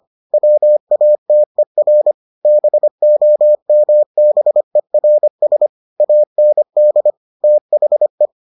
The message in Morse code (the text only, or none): WATER BOMBERS AND THE